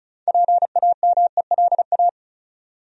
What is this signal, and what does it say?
Morse code, heard as a keyed tone, about 36 wpm, 700 Hz. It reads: PAMELA
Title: USA Common Names - 35wpm